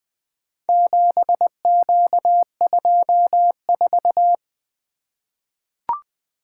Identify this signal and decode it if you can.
Morse code, heard as a keyed tone, about 20 wpm, 700 Hz. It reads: 7Q24
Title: Four Letter-Number Combinations - 20wpm